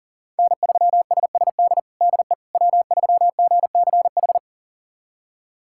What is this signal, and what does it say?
Morse code, heard as a keyed tone, about 40 wpm, 700 Hz. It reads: N3SSD DE W3GCH
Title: USA Callsign DE Callsign - 5 Characters - 40wpm